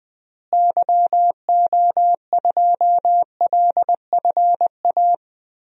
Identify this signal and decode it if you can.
Morse code, heard as a keyed tone, about 21 wpm, 700 Hz. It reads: YO2LFA
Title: DXCC-100 Call Signs - 6 Characters - Repeated 2x - 20wpm